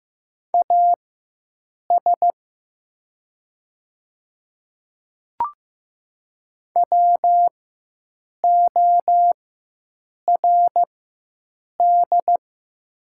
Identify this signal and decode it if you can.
Morse code, heard as a keyed tone, about 15 wpm, spaced as for 7 wpm, 700 Hz. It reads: AS WORD
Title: 4x Character Spacing - Top 100 Words - 15wpm